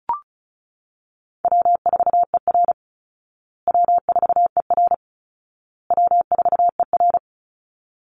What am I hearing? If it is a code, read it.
Morse code, heard as a keyed tone, about 34 wpm, 700 Hz. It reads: W4ER W4ER W4ER
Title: Easy Call Signs Repeated 3x - 35wpm